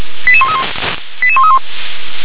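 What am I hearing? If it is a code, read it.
Five-tone selective call (ZVEI), heard as quick runs of five tones, tones of about 70 ms. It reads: 88131 88131
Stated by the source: ZVEI five-tone selective call 88150, received on SDR.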